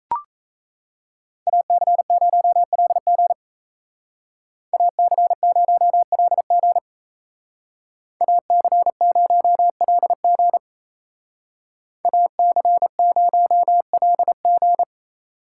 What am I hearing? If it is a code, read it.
Morse code, heard as a keyed tone, about 35 wpm, 700 Hz. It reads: AC0LG AC0LG AC0LG AC0LG
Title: Speed Racing - USA 2x2 Call Signs 28wpm